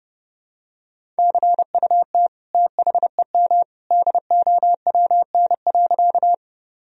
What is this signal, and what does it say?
Morse code, heard as a keyed tone, about 30 wpm, 700 Hz. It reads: CUT THEM DOWN.
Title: Sentences from Top 1000 Words - Review Part 1 - 30wpm